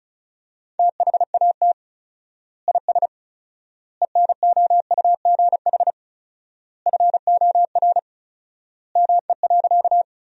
Morse code, heard as a keyed tone, about 34 wpm, 700 Hz. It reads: THAT IS ENOUGH FOR ME.
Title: 4x Word-Spacing - Sentences from Top 400 Words - 35wpm